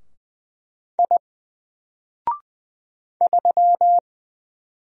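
Morse code, heard as a keyed tone, about 20 wpm, 700 Hz. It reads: I 3
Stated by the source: Single letter and number Morse drill at 20 wpm, from N4LCD.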